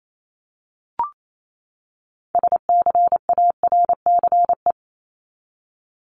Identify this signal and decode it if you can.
Morse code, heard as a keyed tone, about 28 wpm, 700 Hz. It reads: SCARCE
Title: LY Suffix - Top 100 Words with Base Words 28wpm